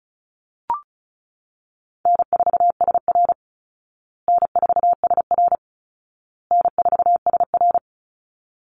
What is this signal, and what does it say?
Morse code, heard as a keyed tone, about 34 wpm, 700 Hz. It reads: N4SR N4SR N4SR
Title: Call Signs Repeated 2x - 35wpm